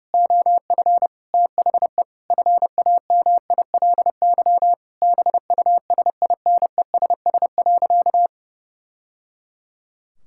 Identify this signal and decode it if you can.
Morse code, heard as a keyed tone, about 30 wpm, 700 Hz. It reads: OF THE FAMILY BUSINESS.
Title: Sentences from Top 1000 Words - Review Part 6 - 30wpm